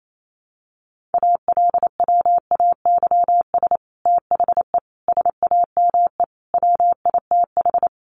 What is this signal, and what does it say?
Morse code, heard as a keyed tone, about 28 wpm, 700 Hz. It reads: ALWAYS THE SAME WITH
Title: Sentences from Top 300 Words 28wpm